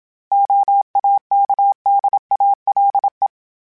Morse code, heard as a keyed tone, about 27 wpm, 800 Hz. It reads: OAKDALE
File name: fromSMs / FromSMs mp3s / Cities@25